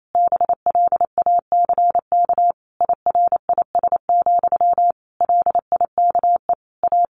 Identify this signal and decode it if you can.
Morse code, heard as a keyed tone, about 28 wpm, 700 Hz. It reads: BLACK IRIS, LIKE A
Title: A Sound Of Thunder 28wpm